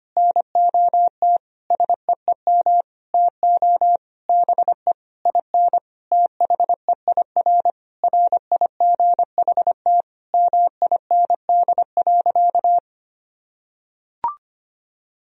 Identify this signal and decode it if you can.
Morse code, heard as a keyed tone, about 25 wpm, 700 Hz. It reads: NOT SEEM TO BE IN THEIR RIGHT MIND.
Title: Sentences from Top 1000 Words - Review Part 1 - 25wpm